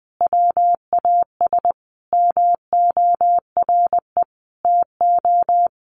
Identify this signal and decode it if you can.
Morse code, heard as a keyed tone, about 20 wpm, 700 Hz. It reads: WAS MORE TO
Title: Sentences from Top 100 Words 20wpm